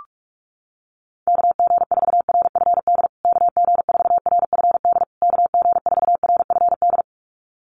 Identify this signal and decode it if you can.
Morse code, heard as a keyed tone, about 44 wpm, 700 Hz. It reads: KG4RFD KG4RFD KG4RFD
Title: USA Call Signs Repeated 3x 45wpm